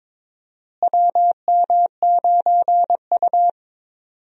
Morse code, heard as a keyed tone, about 22 wpm, 700 Hz. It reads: WM9U
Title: USA Call Signs - 4 Characters - Repeated 2x - 22wpm